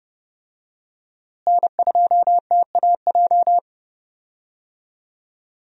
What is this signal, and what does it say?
Morse code, heard as a keyed tone, about 30 wpm, 700 Hz. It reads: N2TAJ
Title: USA Call Signs - 5 Characters - Repeated 2x - 30wpm